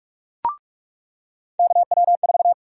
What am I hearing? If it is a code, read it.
Morse code, heard as a keyed tone, about 46 wpm, 700 Hz. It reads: KW4
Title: USA Call Sign Prefixes - 45wpm